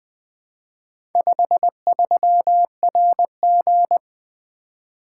Morse code, heard as a keyed tone, about 20 wpm, 700 Hz. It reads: 53RG
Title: Four Letter-Number Combinations - 20wpm